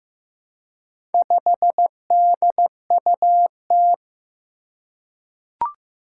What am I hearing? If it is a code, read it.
Morse code, heard as a keyed tone, about 15 wpm, 700 Hz. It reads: 5DUT
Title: Four Challenging Letter-Number Combinations 15wpm